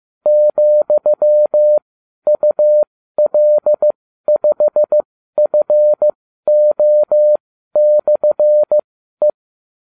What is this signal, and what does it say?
Morse code, heard as a keyed tone, about 15 wpm, spaced as for 12 wpm, 600 Hz. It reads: ,UL5FO/E